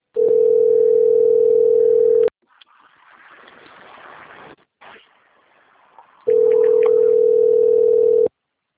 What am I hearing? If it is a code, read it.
Ringback tone (North America), 440 + 480 Hz.
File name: rg-1600-unknown- - - .15807